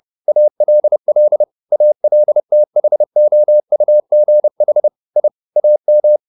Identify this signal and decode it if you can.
Morse code, heard as a keyed tone, about 30 wpm, 600 Hz. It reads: ALL ALTHOUGH I AM
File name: the old man and the sea / the old man and the sea 1 30wpm